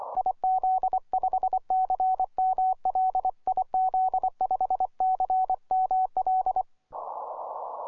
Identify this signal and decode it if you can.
Morse code, heard as a keyed tone, about 24 wpm, 750 Hz. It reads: IZ5CML IZ5CML
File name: simple-file-list / YJ0HA 12M CW